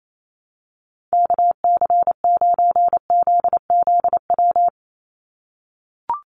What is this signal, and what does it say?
Morse code, heard as a keyed tone, about 28 wpm, 700 Hz. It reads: KC9ZZW
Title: USA Call Signs 28wpm